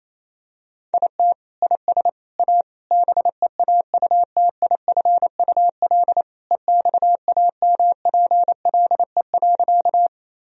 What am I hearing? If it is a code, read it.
Morse code, heard as a keyed tone, about 28 wpm, 700 Hz. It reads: IT IS A BEAUTIFUL EXAMPLE.
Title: Sentences from Top 1000 Words - Review Part 1 - 28wpm